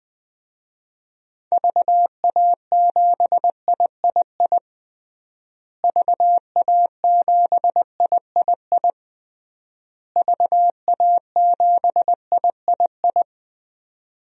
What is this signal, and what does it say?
Morse code, heard as a keyed tone, about 20 wpm, 700 Hz. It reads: VA7III VA7III VA7III
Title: DXCC-100 Call Signs - 6 Characters - Repeated 3x - 20wpm